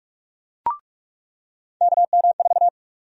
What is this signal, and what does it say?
Morse code, heard as a keyed tone, about 46 wpm, 700 Hz. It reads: KM4
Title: USA Call Sign Prefixes - 45wpm